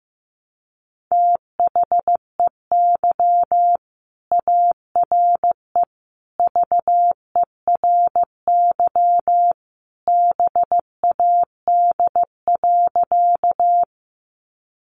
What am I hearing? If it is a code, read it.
Morse code, heard as a keyed tone, about 15 wpm, 700 Hz. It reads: THEY ARE VERY BAD.
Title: Sentences from Top 700 Words 15wpm